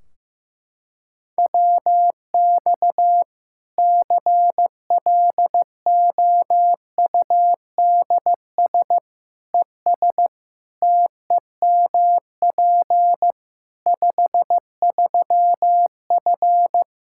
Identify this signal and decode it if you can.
Morse code, heard as a keyed tone, about 15 wpm, 700 Hz. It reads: WX CLOUDS ES TEMP 53F
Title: Practice QSO - Exchange 2 - Info - 15wpm